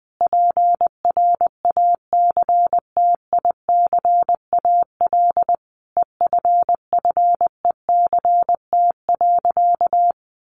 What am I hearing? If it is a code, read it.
Morse code, heard as a keyed tone, about 20 wpm, 700 Hz. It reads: PRACTICAL EFFECT.
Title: Sentences from Top 1400 Words 20wpm